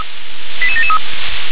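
Five-tone selective call (ZVEI), heard as quick runs of five tones, tones of about 70 ms. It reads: 88773
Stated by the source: ZVEI five-tone call received on SDR.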